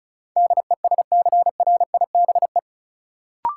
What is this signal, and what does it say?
Morse code, heard as a keyed tone, about 36 wpm, 700 Hz. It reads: DESCRIBE
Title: Sentences from Top 1000 Words - New Word Drill - 35wpm